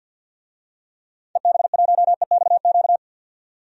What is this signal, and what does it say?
Morse code, heard as a keyed tone, about 52 wpm, 700 Hz. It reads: EB1EXQ
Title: DXCC-100 Call Signs - 6 Characters - 50wpm